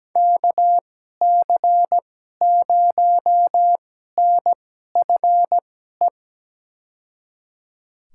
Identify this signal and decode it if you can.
Morse code, heard as a keyed tone, about 17 wpm, spaced as for 12 wpm, 700 Hz. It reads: KC0NFE
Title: USA Call Signs - 6 Characters - 2x Character Spacing - 17wpm